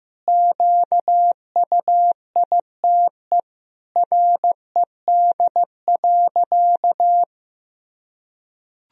Morse code, heard as a keyed tone, about 15 wpm, 700 Hz. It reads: QUITE RED.
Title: Sentences from Top 1000 Words - Review Part 2 - 15wpm